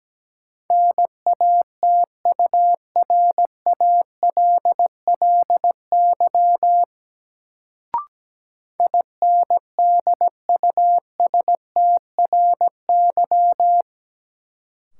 Morse code, heard as a keyed tone, about 17 wpm, 700 Hz. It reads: NATURALLY INDUSTRY
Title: Sentences from Top 1300 Words - New Word Drill Alt - 17wpm